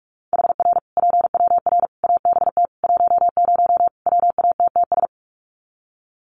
Morse code, heard as a keyed tone, about 44 wpm, 700 Hz. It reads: HR PWR ABT 10 WATTS
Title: Common QSO Phrases 45wpm